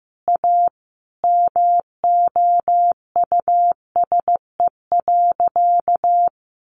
Morse code, heard as a keyed tone, about 15 wpm, 700 Hz. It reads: A MOUSE.
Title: Sentences from Top 700 Words 15wpm